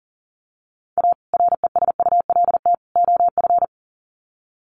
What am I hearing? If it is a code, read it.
Morse code, heard as a keyed tone, about 40 wpm, 700 Hz. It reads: A RESULT OF
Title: Sets of 3 Words 40wpm